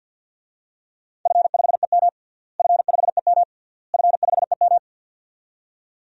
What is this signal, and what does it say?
Morse code, heard as a keyed tone, about 50 wpm, 700 Hz. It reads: F5EM F5EM F5EM
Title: DXCC-100 Call Signs - 4 Characters - Repeated 3x - 50wpm